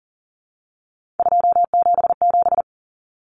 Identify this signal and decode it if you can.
Morse code, heard as a keyed tone, about 40 wpm, 700 Hz. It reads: 277
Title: Common Cut Numbers - 3-Digit 40wpm